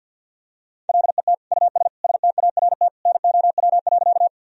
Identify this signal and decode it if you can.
Morse code, heard as a keyed tone, about 45 wpm, 700 Hz. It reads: LET US START NOW.
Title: Sentences from Top 1000 Words - Review Part 1 - 50wpm